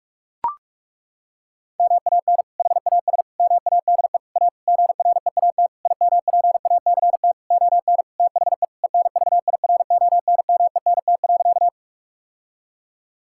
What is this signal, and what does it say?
Morse code, heard as a keyed tone, about 41 wpm, 700 Hz. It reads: MAN HAS MADE A GREAT IMPACT ON THE ENVIRONMENT.
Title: Sentences from Top 1000 Words - Review Part 6 - 45wpm